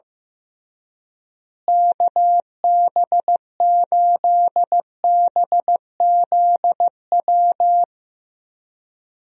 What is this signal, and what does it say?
Morse code, heard as a keyed tone, about 15 wpm, 700 Hz. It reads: KB8BZW
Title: USA Call Signs - 6 Characters - Repeated 3x - 15wpm